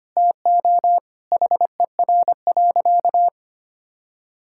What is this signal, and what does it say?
Morse code, heard as a keyed tone, about 25 wpm, 700 Hz. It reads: TO HER.
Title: Sentences from Top 1000 Words - Review Part 6 - 25wpm